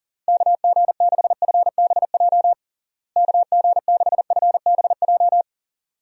Morse code, heard as a keyed tone, about 40 wpm, 700 Hz. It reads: KG6FBJ KG6FBJ
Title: USA Call Signs - 6 Characters - Repeated 2x - 40wpm